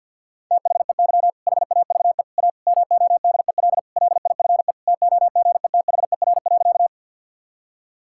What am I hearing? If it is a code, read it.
Morse code, heard as a keyed tone, about 50 wpm, 700 Hz. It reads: THEY HAVE A MODEL LIFE TOGETHER.
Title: Sentences from Top 1000 Words - Review Part 5 - 50wpm